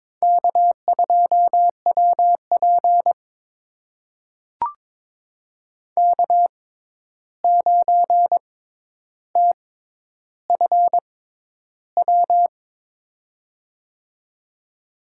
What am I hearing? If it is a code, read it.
Morse code, heard as a keyed tone, about 22 wpm, spaced as for 11 wpm, 700 Hz. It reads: K2WP K9TFW
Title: USA Call Signs - 6x Character Spacing - 22wpm